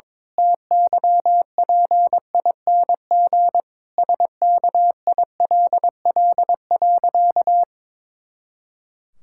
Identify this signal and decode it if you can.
Morse code, heard as a keyed tone, about 22 wpm, 700 Hz. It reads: TYPING SKILL.
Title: Pangram Challenge v2 - 22wpm